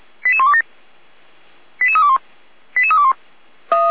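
Five-tone selective call (ZVEI), heard as quick runs of five tones, tones of about 70 ms. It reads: 88127 88321 88321
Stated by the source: ZVEI five-tone call 88352 received by SDR.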